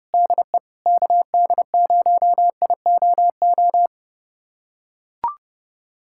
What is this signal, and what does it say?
Morse code, heard as a keyed tone, about 30 wpm, 700 Hz. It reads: DE KD0IOO
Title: USA Callsign DE Callsign - 6 Characters - 30wpm